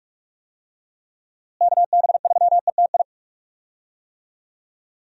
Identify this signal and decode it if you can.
Morse code, heard as a keyed tone, about 41 wpm, 700 Hz. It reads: KB3ETI
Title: USA Call Signs - 6 Characters - Repeated 3x - 45wpm